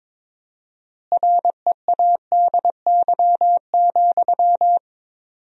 Morse code, heard as a keyed tone, about 22 wpm, 700 Hz. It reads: READY,
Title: 4x Word-Spacing - Sentences from Top 500 Words - 22wpm